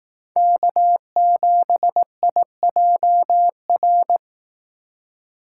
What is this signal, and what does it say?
Morse code, heard as a keyed tone, about 18 wpm, 700 Hz. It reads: K7IJR
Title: Speed Racing - USA 1x3 Call Signs 15wpm